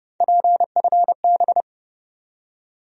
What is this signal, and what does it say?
Morse code, heard as a keyed tone, about 30 wpm, 700 Hz. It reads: PFB
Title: USA Call Sign Postfixes - 3 Letters 30wpm